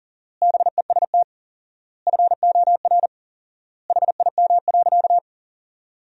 Morse code, heard as a keyed tone, about 40 wpm, 700 Hz. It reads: BEST FOR HIM.
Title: 4x Word-Spacing - Sentences from Top 500 Words - 40wpm